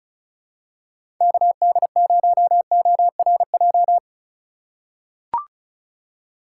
Morse code, heard as a keyed tone, about 36 wpm, 700 Hz. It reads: KD0ORJ
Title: USA Call Signs - 6x Character Spacing - 35wpm